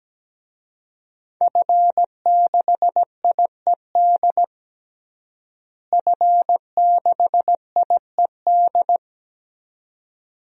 Morse code, heard as a keyed tone, about 17 wpm, 700 Hz. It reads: F6IED F6IED
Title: DXCC-100 Call Signs - 5 Characters - Repeated 2x - 17wpm